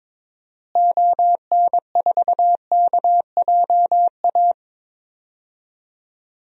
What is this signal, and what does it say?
Morse code, heard as a keyed tone, about 22 wpm, 700 Hz. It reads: ON4KJA
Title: DXCC-100 Call Signs - 6 Characters - Repeated 2x - 22wpm